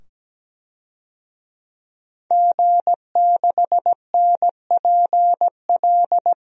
Morse code, heard as a keyed tone, about 17 wpm, 700 Hz. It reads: G6NPL
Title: DXCC-100 Call Signs - 5 Characters - 17wpm